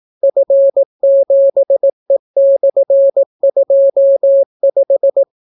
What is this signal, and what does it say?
Morse code, heard as a keyed tone, about 18 wpm, 550 Hz. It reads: F7E/25